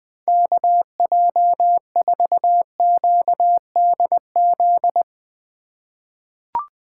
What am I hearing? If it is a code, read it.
Morse code, heard as a keyed tone, about 20 wpm, 700 Hz. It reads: KJ4QDZ
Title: USA Call Signs - 6 Characters - 2x Character Spacing - 20wpm